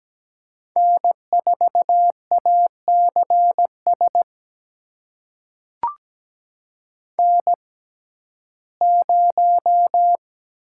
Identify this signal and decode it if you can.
Morse code, heard as a keyed tone, about 17 wpm, spaced as for 12 wpm, 700 Hz. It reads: N4ACS N0
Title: USA Call Signs - 6x Character Spacing - 17wpm